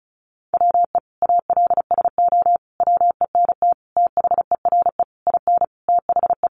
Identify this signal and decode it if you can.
Morse code, heard as a keyed tone, about 34 wpm, 700 Hz. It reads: WE ALSO WENT THERE IN THE
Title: Sentences from Top 1200 Words 35wpm